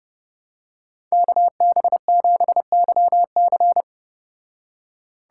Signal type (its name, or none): Morse code